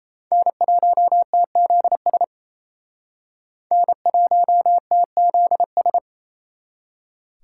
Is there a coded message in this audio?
Yes: Morse code from 0.3 to 6.0 s, heard as a keyed tone, about 35 wpm, 700 Hz. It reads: N1TZS N1TZS